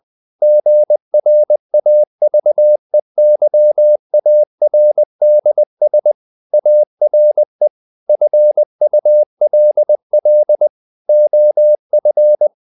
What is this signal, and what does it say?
Morse code, heard as a keyed tone, about 20 wpm, 600 Hz. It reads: GRAVEYARDS ARE FULL OF
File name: CW / Practice / EnglishQuotations0320